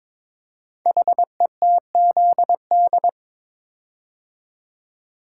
Morse code, heard as a keyed tone, about 22 wpm, 700 Hz. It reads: HETZD
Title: Code Groups - 5 Letters - 22wpm